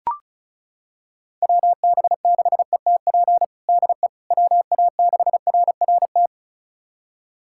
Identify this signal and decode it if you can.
Morse code, heard as a keyed tone, about 36 wpm, 700 Hz. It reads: WB6ETP DE WA6RRT
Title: USA Callsign DE Callsign - 6 Characters - 35wpm